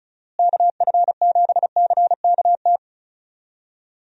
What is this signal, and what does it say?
Morse code, heard as a keyed tone, about 36 wpm, 700 Hz. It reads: KF7CKT
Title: USA Call Signs - 6 Characters - Repeated 2x - 35wpm